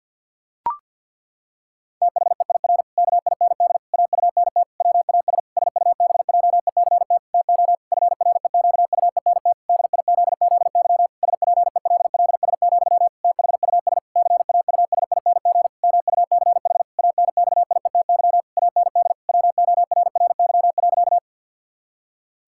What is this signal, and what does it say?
Morse code, heard as a keyed tone, about 50 wpm, 700 Hz. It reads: THEIR KIND AUNT WAS SUBJECT TO FREQUENT DIZZY SPELLS, THUS CAUSING MUCH ANXIETY AND WORRY.
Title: Pangram Challenge v2 - 50wpm